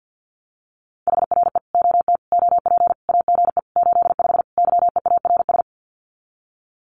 Morse code, heard as a keyed tone, about 50 wpm, 700 Hz. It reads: HRE OT OP AGE 85 YEARS
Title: Common QSO Phrases 50wpm